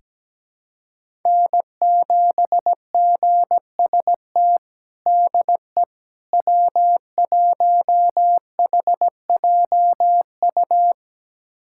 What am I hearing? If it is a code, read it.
Morse code, heard as a keyed tone, about 17 wpm, 700 Hz. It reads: N7GST DE W1HJU
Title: USA Callsign DE Callsign - 5 Characters - 17wpm